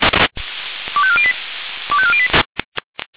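Five-tone selective call (ZVEI), heard as quick runs of five tones, tones of about 70 ms. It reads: 25408 25408
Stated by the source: ZVEI five-tone selective call received on SDR.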